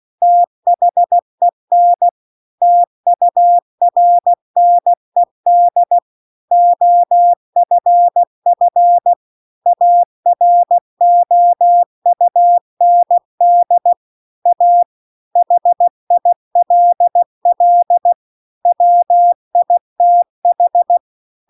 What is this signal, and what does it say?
Morse code, heard as a keyed tone, about 16 wpm, 700 Hz. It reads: THEN TURNED OFF AROUND A HILL WITH